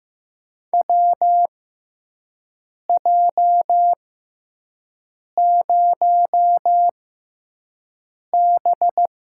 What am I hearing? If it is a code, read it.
Morse code, heard as a keyed tone, about 15 wpm, spaced as for 5 wpm, 700 Hz. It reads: WJ0B
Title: USA Call Signs - 6x Character Spacing - 15wpm